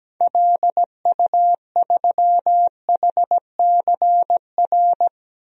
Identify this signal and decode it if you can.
Morse code, heard as a keyed tone, about 17 wpm, 700 Hz. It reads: LU3HCR
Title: DXCC-100 Call Signs - 6 Characters - 17wpm